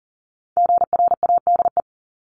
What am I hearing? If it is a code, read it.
Morse code, heard as a keyed tone, about 40 wpm, 700 Hz. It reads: GRADE
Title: Top 1501-1750 Words 40wpm